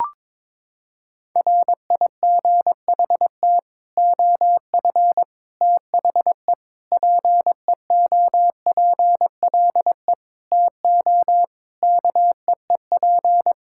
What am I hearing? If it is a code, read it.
Morse code, heard as a keyed tone, about 22 wpm, 700 Hz. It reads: RIGHT OF THE PEOPLE TO KEEP
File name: practice / 22wpm / Sets of 8 Words - 22wpm